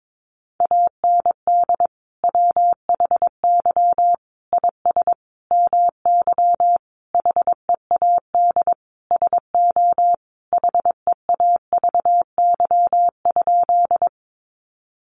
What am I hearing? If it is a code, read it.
Morse code, heard as a keyed tone, about 22 wpm, 700 Hz. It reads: AND WHY IS MY HEAD SO HEAVY?
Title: Sentences from Top 1100 Words 22wpm